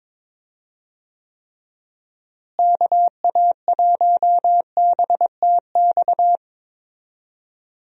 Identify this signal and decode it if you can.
Morse code, heard as a keyed tone, about 22 wpm, 700 Hz. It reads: KA1BTX
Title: USA Call Signs - 6 Characters - Repeated 2x - 22wpm